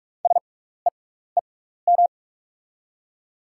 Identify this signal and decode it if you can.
Morse code, heard as a keyed tone, about 47 wpm, spaced as for 16 wpm, 690 Hz. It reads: SEEM
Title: 6x Character Spacing - Top 300 Words - 45wpm